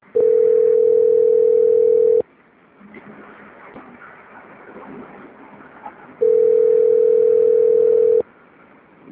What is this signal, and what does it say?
Ringback tone (North America), 440 + 480 Hz.
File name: out- - - - - .26514